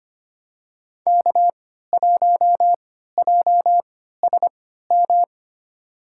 Morse code, heard as a keyed tone, about 25 wpm, spaced as for 14 wpm, 700 Hz. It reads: K1JSM
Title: USA Call Signs - 5 Characters - 3x Character Spacing - 25wpm